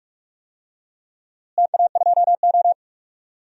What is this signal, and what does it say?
Morse code, heard as a keyed tone, about 47 wpm, 700 Hz. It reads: TA2O
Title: DXCC-100 Call Signs - 4 Characters - Repeated 2x - 45wpm